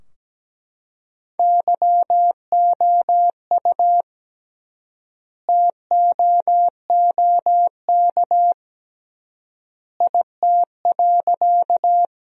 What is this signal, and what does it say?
Morse code, heard as a keyed tone, about 17 wpm, 700 Hz. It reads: YOU TOOK IT.
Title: 3x Word-Spacing - Sentences from Top 300 Words - 17wpm